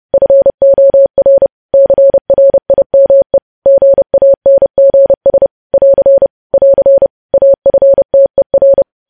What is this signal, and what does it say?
Morse code, heard as a keyed tone, about 30 wpm, 550 Hz. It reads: FOR CRIME GANGS <AR> <AR> AFTER
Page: news headlines in morse code at 30 wpm